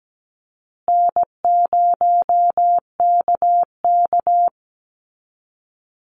Morse code, heard as a keyed tone, about 17 wpm, 700 Hz. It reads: N0KK
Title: Easy Call Signs Repeated 3x - 17wpm